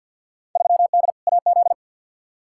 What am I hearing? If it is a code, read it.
Morse code, heard as a keyed tone, about 50 wpm, 700 Hz. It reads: 3D AZ
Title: Field Day Exchanges - 50wpm